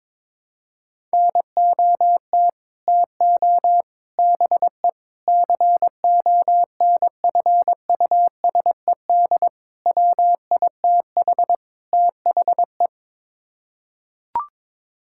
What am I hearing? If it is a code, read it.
Morse code, heard as a keyed tone, about 22 wpm, 700 Hz. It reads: NOT TO BE CONFUSED WITH THE
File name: practice / 22wpm / Sets of 6 Words - 22wpm